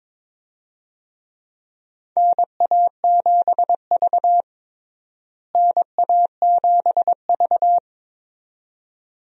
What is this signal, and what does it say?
Morse code, heard as a keyed tone, about 22 wpm, 700 Hz. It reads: NA7V NA7V
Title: USA Call Signs - 4 Characters - Repeated 2x - 22wpm